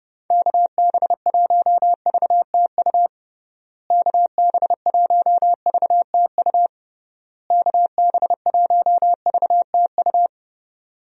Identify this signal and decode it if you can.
Morse code, heard as a keyed tone, about 30 wpm, 700 Hz. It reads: KB1VTU KB1VTU KB1VTU
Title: USA Call Signs - 6 Characters - Repeated 3x - 30wpm